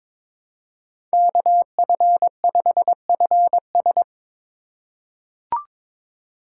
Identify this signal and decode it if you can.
Morse code, heard as a keyed tone, about 22 wpm, 700 Hz. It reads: KF5FS
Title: USA Call Signs - 5 Characters - 2x Character Spacing - 22wpm